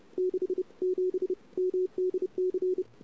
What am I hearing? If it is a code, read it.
Morse code, heard as a keyed tone, about 31 wpm, 380 Hz. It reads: 67 MDC
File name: CW / 2008-ss-cw-k5tr-op-n5rz -22:00